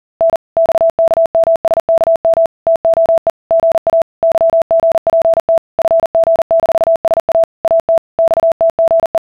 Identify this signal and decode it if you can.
Morse code (Russian alphabet), heard as a keyed tone, about 40 wpm, 660 Hz. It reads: Н ЬКМСКМ ТОЕ ГА ЫГПТ ФГ-СА АТ ЬТГЕ